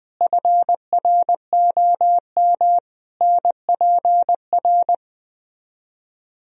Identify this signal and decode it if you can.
Morse code, heard as a keyed tone, about 20 wpm, 700 Hz. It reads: FROM NPR